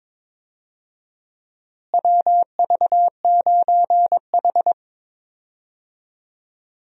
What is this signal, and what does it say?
Morse code, heard as a keyed tone, about 22 wpm, 700 Hz. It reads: WV9H
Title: USA Call Signs - 4 Characters - Repeated 3x - 22wpm